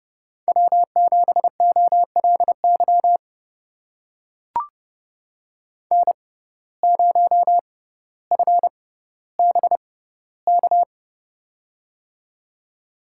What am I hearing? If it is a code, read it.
Morse code, heard as a keyed tone, about 30 wpm, spaced as for 15 wpm, 700 Hz. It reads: W7OLY N0FBK
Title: USA Call Signs - 5 Characters - 6x Character Spacing - 30wpm